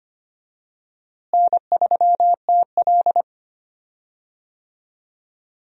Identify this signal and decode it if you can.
Morse code, heard as a keyed tone, about 25 wpm, 700 Hz. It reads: N3TL
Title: USA Call Signs - 4 Characters - Repeated 2x - 25wpm